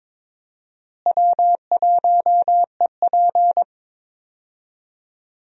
Morse code, heard as a keyed tone, about 22 wpm, 700 Hz. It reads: W1EP
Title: USA Call Signs - 4 Characters - Repeated 3x - 22wpm